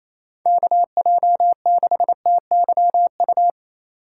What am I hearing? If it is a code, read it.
Morse code, heard as a keyed tone, about 34 wpm, 700 Hz. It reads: KJ6TYU
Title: Speed Racing - USA Call Signs 28wpm